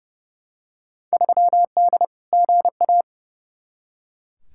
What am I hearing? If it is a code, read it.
Morse code, heard as a keyed tone, about 30 wpm, 700 Hz. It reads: 3D GA
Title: Field Day Exchanges Encore - 30wpm